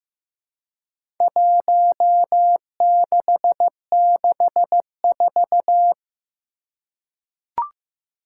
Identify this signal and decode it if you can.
Morse code, heard as a keyed tone, about 15 wpm, 700 Hz. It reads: 1664
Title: Four Challenging Letter-Number Combinations 15wpm